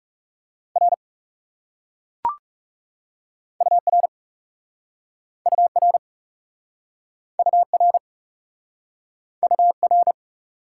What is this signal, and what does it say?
Morse code, heard as a keyed tone, about 37 wpm, 700 Hz. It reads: R UR UR UR UR
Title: Speed-Racing Top 90 QSO Elements 30wpm